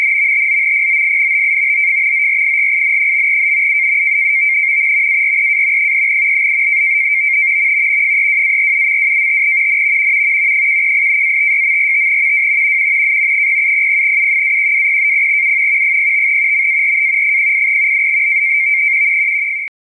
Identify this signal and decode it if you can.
NAVTEX (SITOR-B), about 100 baud. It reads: TO POINT OF ORIGIN. AREA 3 32-47.25 N 118-56.45 W 0700U-1200U 05 MAR 08